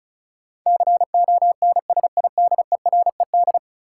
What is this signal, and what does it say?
Morse code, heard as a keyed tone, about 36 wpm, 700 Hz. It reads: CONSIDERED
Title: Sets of 5 Words - New Word Drill - 35wpm